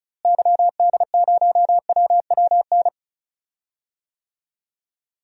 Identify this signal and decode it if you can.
Morse code, heard as a keyed tone, about 36 wpm, 700 Hz. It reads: YD0WWN
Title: DXCC-100 Call Signs - 6 Characters - Repeated 3x - 35wpm